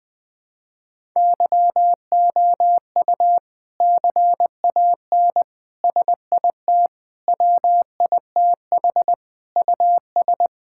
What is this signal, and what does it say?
Morse code, heard as a keyed tone, about 21 wpm, 700 Hz. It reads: YOU CAN SIT WITH US
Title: Sentences from Top 1000 Words - Review Part 5 - 20wpm